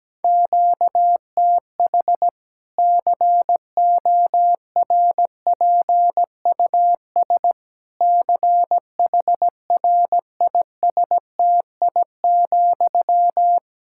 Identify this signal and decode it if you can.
Morse code, heard as a keyed tone, about 17 wpm, 700 Hz. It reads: QTH CORPUS CHRISTI,
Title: K1USN SST QSO Practice 17wpm